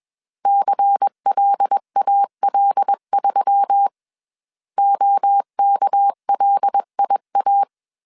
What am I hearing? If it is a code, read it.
Morse code, heard as a keyed tone, about 21 wpm, 800 Hz. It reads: CLAL3 OKLIA
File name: CW / DET / 1 6 1500C S A 20wpm 2048